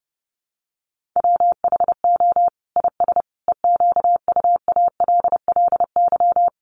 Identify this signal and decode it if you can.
Morse code, heard as a keyed tone, about 30 wpm, 700 Hz. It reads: WHO IS EQUALLY
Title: Sentences from Top 1400 Words 30wpm